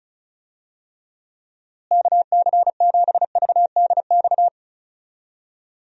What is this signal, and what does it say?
Morse code, heard as a keyed tone, about 34 wpm, 700 Hz. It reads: KC7VDX
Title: USA Call Signs - 6 Characters - Repeated 3x - 35wpm